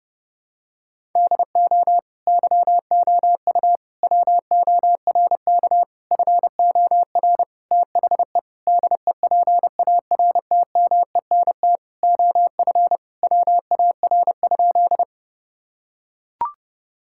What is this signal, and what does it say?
Morse code, heard as a keyed tone, about 30 wpm, 700 Hz. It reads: DO YOU WORK FOR THE DEPARTMENT OF WAR?
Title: Sentences from Top 1000 Words - Review Part 5 - 30wpm